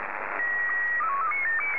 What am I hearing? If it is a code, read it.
SSTV (slow-scan television), Scottie 1.